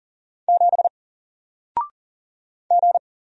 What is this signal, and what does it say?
Morse code, heard as a keyed tone, about 40 wpm, 700 Hz. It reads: 7 G
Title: Single Letter-Number 40wpm